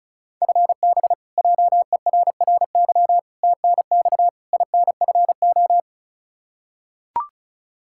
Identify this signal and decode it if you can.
Morse code, heard as a keyed tone, about 36 wpm, 700 Hz. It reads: FB JERRY TNX INFO
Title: K1USN SST QSO Practice 35wpm